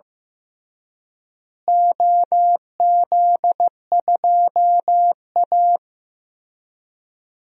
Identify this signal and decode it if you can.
Morse code, heard as a keyed tone, about 15 wpm, 700 Hz. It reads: OZ2A
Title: DXCC-100 Call Signs - 4 Characters - Repeated 3x - 15wpm